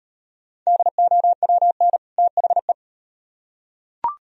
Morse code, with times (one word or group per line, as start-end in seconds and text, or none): 0.7-2.0 DOWN
2.2-2.7 THE